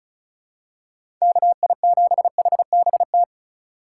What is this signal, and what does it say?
Morse code, heard as a keyed tone, about 36 wpm, 700 Hz. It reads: KI7HBT
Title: USA Call Signs - 6 Characters - 2x Character Spacing - 35wpm